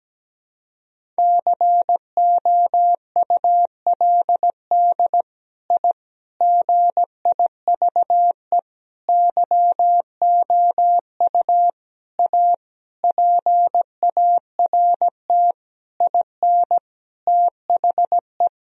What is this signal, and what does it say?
Morse code, heard as a keyed tone, about 17 wpm, 700 Hz. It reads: COULD I GIVE YOU A PART IN THE
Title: Sentences from Top 1000 Words - Review Part 1 - 17wpm